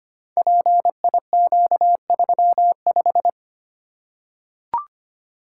Morse code, heard as a keyed tone, about 25 wpm, 700 Hz. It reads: PIQ35
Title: Code Groups - 5 Letter:Numbers - 25wpm